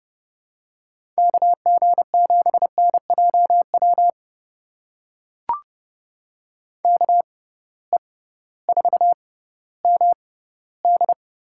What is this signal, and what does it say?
Morse code, heard as a keyed tone, about 30 wpm, spaced as for 16 wpm, 700 Hz. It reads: KG7NJW KE4MD
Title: USA Call Signs - 6 Characters - 6x Character Spacing - 30wpm